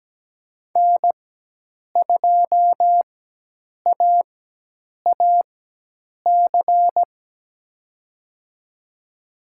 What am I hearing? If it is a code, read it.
Morse code, heard as a keyed tone, about 17 wpm, spaced as for 8 wpm, 700 Hz. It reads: N2AAC
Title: USA Call Signs - 5 Characters - 4x Character Spacing - 17wpm